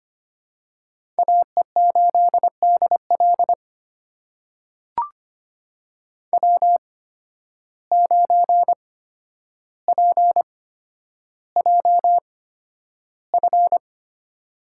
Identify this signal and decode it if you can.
Morse code, heard as a keyed tone, about 25 wpm, spaced as for 11 wpm, 700 Hz. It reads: AE8DL W9PJF
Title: USA Call Signs - 5 Characters - 8x Character Spacing - 25wpm